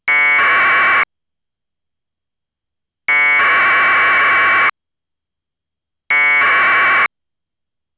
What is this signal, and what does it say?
Packet radio (AX.25, as in APRS).